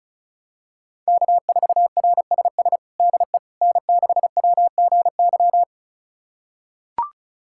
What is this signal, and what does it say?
Morse code, heard as a keyed tone, about 36 wpm, 700 Hz. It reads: K4RSS DE N6WGY
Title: USA Callsign DE Callsign 35wpm